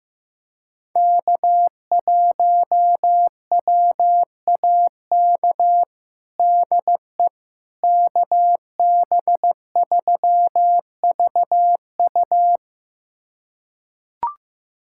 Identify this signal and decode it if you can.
Morse code, heard as a keyed tone, about 15 wpm, 700 Hz. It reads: K1WAK DE KB3VU
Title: USA Callsign DE Callsign - 5 Characters - 15wpm